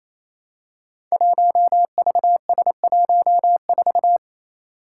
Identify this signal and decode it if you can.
Morse code, heard as a keyed tone, about 27 wpm, 700 Hz. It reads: 1VS14
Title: Five Challenging Letter-Number Combinations 28wpm